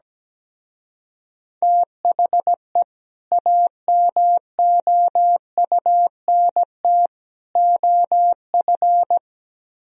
Morse code, heard as a keyed tone, about 17 wpm, 700 Hz. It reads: THE AMOUNT OF
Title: Speed-Racing - Sets of 3 Words 17wpm